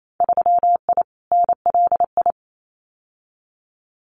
Morse code, heard as a keyed tone, about 28 wpm, 700 Hz. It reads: 3I NLI
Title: Winter Field Day Exchanges 28wpm